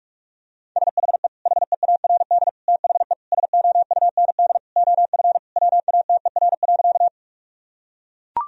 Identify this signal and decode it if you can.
Morse code, heard as a keyed tone, about 42 wpm, 700 Hz. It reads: SHE HEARD THE SOUND OF WATER.
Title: Sentences from Top 1000 Words - Review Part 1 - 45wpm